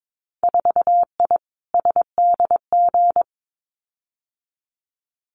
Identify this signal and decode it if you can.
Morse code, heard as a keyed tone, about 22 wpm, 700 Hz. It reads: 4I SDG
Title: Winter Field Day Exchanges Encore 22wpm